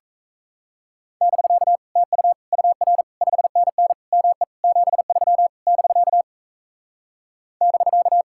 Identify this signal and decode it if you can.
Morse code, heard as a keyed tone, about 37 wpm, 700 Hz. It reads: <BK> TU UR 5NN ME 73 <BK> <BK>